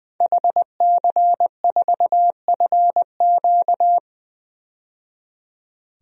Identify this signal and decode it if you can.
Morse code, heard as a keyed tone, about 20 wpm, 700 Hz. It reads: HC4FQ
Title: DXCC-100 Call Signs - 5 Characters - 20wpm